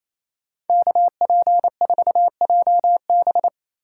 Morse code, heard as a keyed tone, about 37 wpm, 700 Hz. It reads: KP4JB